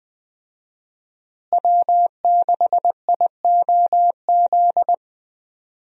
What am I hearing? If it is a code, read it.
Morse code, heard as a keyed tone, about 20 wpm, 700 Hz. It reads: W6IOZ
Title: USA Call Signs - 5 Characters - Repeated 2x - 20wpm